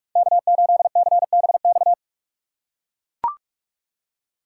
Morse code, heard as a keyed tone, about 41 wpm, 700 Hz. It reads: K8CBX
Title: USA Call Signs - 5 Characters - 3x Character Spacing - 45wpm